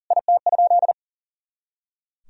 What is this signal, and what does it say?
Morse code, heard as a keyed tone, about 37 wpm, 700 Hz. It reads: IT?
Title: Sentences from Top 100 Words and Abbreviations 40wpm